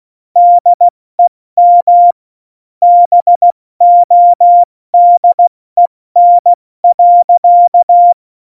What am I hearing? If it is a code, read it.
Morse code, heard as a keyed tone, about 16 wpm, 700 Hz. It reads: DEM BODEN.